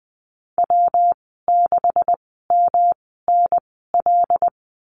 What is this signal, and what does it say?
Morse code, heard as a keyed tone, about 20 wpm, spaced as for 14 wpm, 700 Hz. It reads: W6MNL
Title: 128 Lesson 21g - Number 6 - Call Signs 20wpm